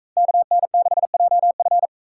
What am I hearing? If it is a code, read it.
Morse code, heard as a keyed tone, about 39 wpm, 700 Hz. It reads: KN6JF